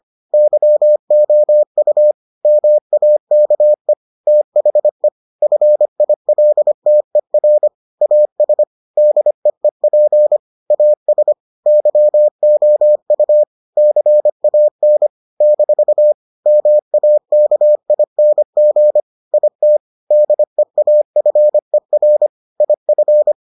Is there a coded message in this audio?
Yes: Morse code from 0.3 to 23.3 s, heard as a keyed tone, about 25 wpm, 600 Hz. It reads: YOU MAKE THE FILTER AS DEEP AS YOU CAN - MAKING IT DEAFER IF